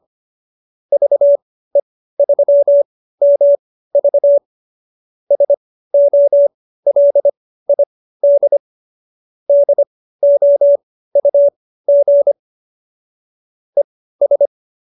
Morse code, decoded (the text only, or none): VE3MV SOLID DOUG ES